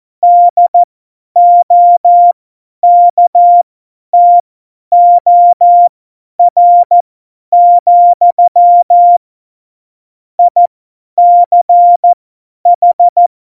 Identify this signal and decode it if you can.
Morse code, heard as a keyed tone, about 14 wpm, spaced as for 10 wpm, 700 Hz. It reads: DOKTOR, ICH
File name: Morsekurs / Witze / Witz 23. Januar 2018, 50 F70